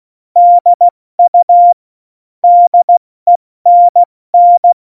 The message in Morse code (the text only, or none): DU DENN